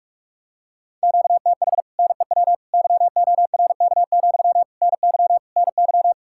Morse code, heard as a keyed tone, about 49 wpm, 700 Hz. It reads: QTH NEW YORK, NY NY